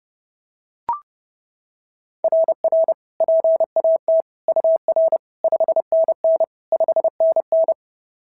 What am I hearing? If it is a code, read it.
Morse code, heard as a keyed tone, about 26 wpm, 650 Hz. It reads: RR PAT UR 5NN 5NN
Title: SOTA QSO Practice 30wpm